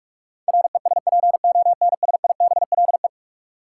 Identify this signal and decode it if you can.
Morse code, heard as a keyed tone, about 46 wpm, 700 Hz. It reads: RESPONSIBLE
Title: Sentences from Top 1600 Words - New Word Drill Alt - 45wpm